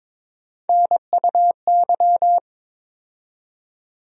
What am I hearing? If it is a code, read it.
Morse code, heard as a keyed tone, about 22 wpm, 700 Hz. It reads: NUY
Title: USA Call Sign Postfixes - 3 Letters 22wpm